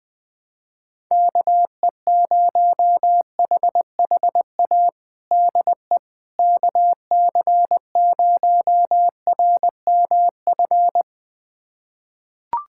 Morse code, heard as a keyed tone, about 20 wpm, 700 Hz. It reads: KE0HHA DE KC0RMF
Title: USA Callsign DE Callsign 20wpm